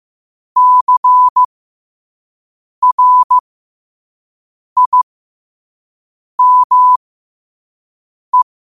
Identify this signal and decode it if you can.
Morse code, heard as a keyed tone, about 15 wpm, spaced as for 5 wpm, 1000 Hz. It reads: CRIME
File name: qotdfiles / 5wpm